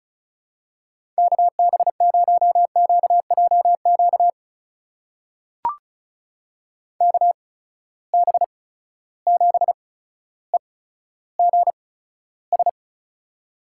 Morse code, heard as a keyed tone, about 36 wpm, spaced as for 15 wpm, 700 Hz. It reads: KB0QJQ KB7EGS
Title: USA Call Signs - 6 Characters - 8x Character Spacing - 35wpm